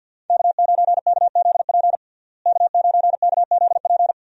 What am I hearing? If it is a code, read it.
Morse code, heard as a keyed tone, about 51 wpm, 700 Hz. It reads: K9KZP K9KZP
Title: USA Call Signs - 5 Characters - Repeated 3x - 50wpm